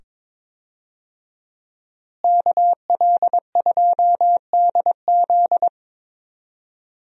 Morse code, heard as a keyed tone, about 22 wpm, 700 Hz. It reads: KL2DZ
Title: USA Call Signs - 5 Characters - Repeated 2x - 22wpm